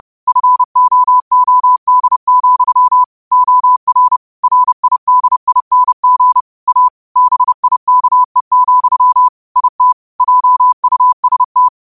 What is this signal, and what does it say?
Morse code, heard as a keyed tone, about 30 wpm, 1000 Hz. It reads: FOOD, OR RIDING A BIKE, IT JUST
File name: qotdfiles / 30wpm